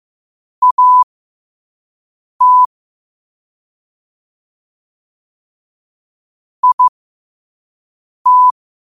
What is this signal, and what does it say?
Morse code, heard as a keyed tone, about 15 wpm, spaced as for 5 wpm, 1000 Hz. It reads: AT IT